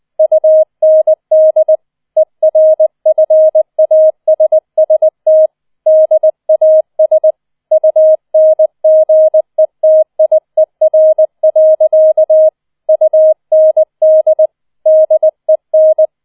Morse code, heard as a keyed tone, about 20 wpm, 630 Hz. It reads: UND ERFASST DAS UNGETIER. UND DEN
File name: 20wpm